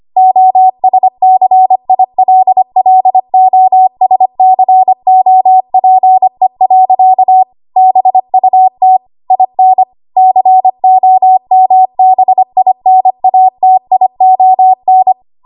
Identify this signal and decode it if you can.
Morse code, heard as a keyed tone, about 25 wpm, 750 Hz. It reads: OSCILLOSCOPE. BUT IN COMBINATION
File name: Archive / 25 WPM / 25WPM